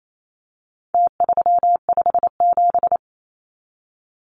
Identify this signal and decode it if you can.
Morse code, heard as a keyed tone, about 28 wpm, 700 Hz. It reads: T357
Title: Numbers - Contest Style - 4-Digit 28wpm